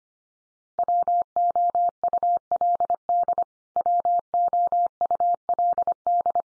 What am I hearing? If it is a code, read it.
Morse code, heard as a keyed tone, about 25 wpm, 700 Hz. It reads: WOULD WOULD
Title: Top 100 Words 25wpm x3